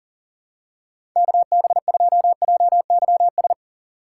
Morse code, heard as a keyed tone, about 38 wpm, 700 Hz. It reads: KB2JYS
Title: USA Call Signs - 6 Characters - 2x Character Spacing - 40wpm